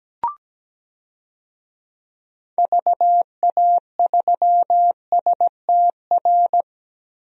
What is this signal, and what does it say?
Morse code, heard as a keyed tone, about 17 wpm, 700 Hz. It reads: VA3STR
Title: DXCC-100 Call Signs - 6 Characters - 17wpm